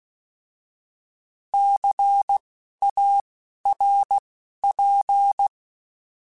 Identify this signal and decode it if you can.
Morse code, heard as a keyed tone, about 16 wpm, spaced as for 12 wpm, 800 Hz. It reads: CARP
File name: Alphabetical / Alphabetical mp3s / Cs @10